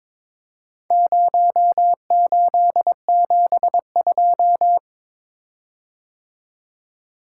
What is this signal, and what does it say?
Morse code, heard as a keyed tone, about 22 wpm, 700 Hz. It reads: 0872
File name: practice / 22wpm / Four Letter-Number Combinations - 22wpm